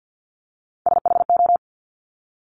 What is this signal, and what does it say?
Morse code, heard as a keyed tone, about 50 wpm, 700 Hz. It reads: SHO
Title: Top Three Letter Combinations in Words 50wpm